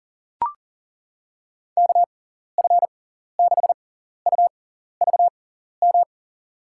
Morse code, heard as a keyed tone, about 39 wpm, spaced as for 14 wpm, 700 Hz. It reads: KF6UVM
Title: USA Call Signs - 6 Characters - 6x Character Spacing - 40wpm